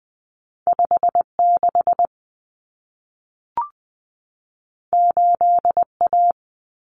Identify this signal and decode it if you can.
Morse code, heard as a keyed tone, about 20 wpm, 700 Hz. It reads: 56 8A
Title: Common Cut Numbers - 2-Digit 20wpm